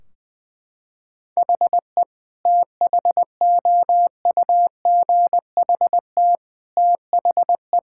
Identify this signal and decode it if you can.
Morse code, heard as a keyed tone, about 20 wpm, 700 Hz. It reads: HE THOUGHT THE
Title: Sentences from Top 1600 Words 20wpm